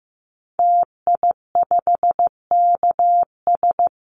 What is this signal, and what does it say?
Morse code, heard as a keyed tone, about 15 wpm, 700 Hz. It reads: TI5KS
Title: Call Signs Repeated 2x - 15wpm